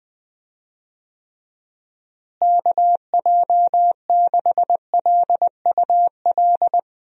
Morse code, heard as a keyed tone, about 20 wpm, 700 Hz. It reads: KJ6LUL
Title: USA Call Signs - 6 Characters - Repeated 2x - 20wpm